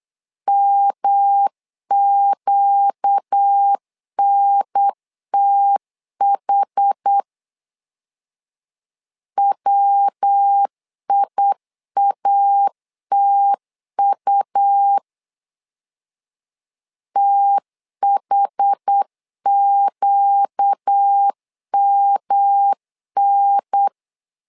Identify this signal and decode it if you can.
Morse code, heard as a keyed tone, about 8 wpm, 800 Hz. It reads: MQNTH WIATU THQMN
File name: LEZIONI / 4 Q 8wpm 2732